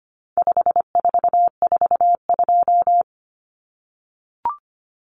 Morse code, heard as a keyed tone, about 25 wpm, 700 Hz. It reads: 5442
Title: Numbers - 4-Digit 25wpm